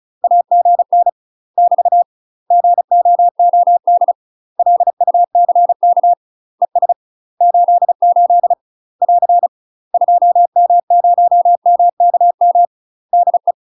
Morse code, heard as a keyed tone, about 34 wpm, 700 Hz. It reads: AGN <BT> GOOD LUCK ES 88 <AR> 2M0MKM DE
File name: UK MASTER QSOs TRAINING / 35WPM / UK MASTER QSOs 0008